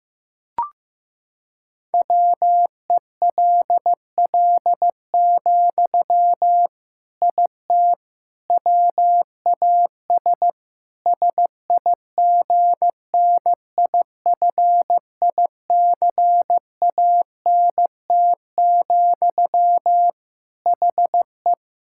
Morse code, heard as a keyed tone, about 15 wpm, 700 Hz. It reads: WELL, IT WAS SIGNIFICANT, HE
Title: Sentences from Top 1000 Words - Review Part 6 - 15wpm